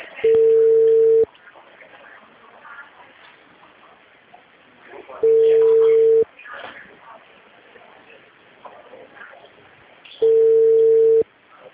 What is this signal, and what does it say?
Ringback tone (China, 450 Hz), with speech.
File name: out- -2020